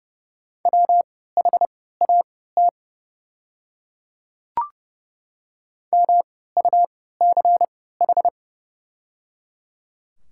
Morse code, heard as a keyed tone, about 30 wpm, spaced as for 17 wpm, 690 Hz. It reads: WHAT MUCH
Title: 3x Character Spacing - Top 200 Words - 30wpm